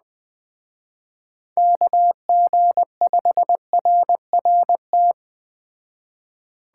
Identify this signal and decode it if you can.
Morse code, heard as a keyed tone, about 21 wpm, 700 Hz. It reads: KG5RRT